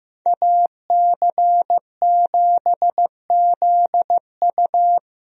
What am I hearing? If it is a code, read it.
Morse code, heard as a keyed tone, about 15 wpm, 700 Hz. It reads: AC7ZU
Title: USA Call Signs - 5 Characters - Repeated 3x - 15wpm